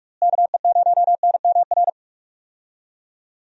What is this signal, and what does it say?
Morse code, heard as a keyed tone, about 43 wpm, 700 Hz. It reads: KE0NMR
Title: USA Call Signs - 6 Characters - Repeated 2x - 45wpm